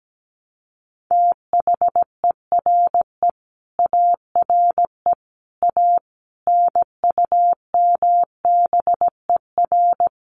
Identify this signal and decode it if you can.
Morse code, heard as a keyed tone, about 17 wpm, 700 Hz. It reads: THERE ARE A NUMBER
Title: Sets of 5 Words 17wpm